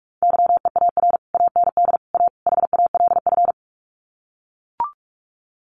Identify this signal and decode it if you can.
Morse code, heard as a keyed tone, about 44 wpm, 700 Hz. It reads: YEAR AND A HALF
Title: Sets of 4 Words 45wpm